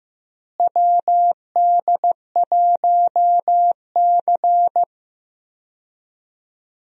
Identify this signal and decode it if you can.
Morse code, heard as a keyed tone, about 15 wpm, 700 Hz. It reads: WD1C
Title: USA Call Signs - 4 Characters - Repeated 2x - 15wpm